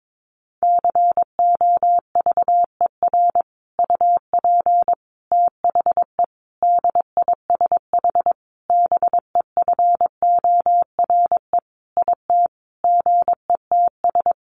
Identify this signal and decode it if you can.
Morse code, heard as a keyed tone, about 22 wpm, 700 Hz. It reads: COVER UP THE DISH BEFORE IT GETS
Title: Sentences from Top 1300 Words 22wpm